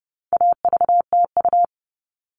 Morse code, heard as a keyed tone, about 30 wpm, 700 Hz. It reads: AVTU
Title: All Cut Numbers - 4-Digit 30wpm